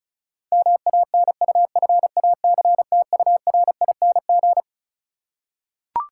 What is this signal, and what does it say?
Morse code, heard as a keyed tone, about 36 wpm, 700 Hz. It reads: MANUFACTURING
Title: Practice QSO - Exchange 3 - More Info - 35wpm